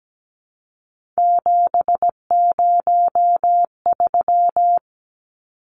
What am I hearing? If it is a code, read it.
Morse code, heard as a keyed tone, about 17 wpm, 700 Hz. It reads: 703
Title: Numbers - 3-Digit 17wpm